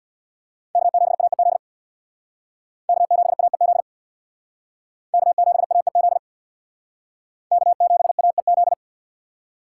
Morse code, heard as a keyed tone, about 62 wpm, 700 Hz. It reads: K7AEZ K7AEZ K7AEZ K7AEZ
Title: Speed Racing - USA Call Signs 50wpm